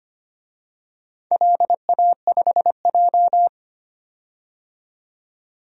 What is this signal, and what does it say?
Morse code, heard as a keyed tone, about 25 wpm, 700 Hz. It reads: LA5J
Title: DXCC-100 Call Signs - 4 Characters - 25wpm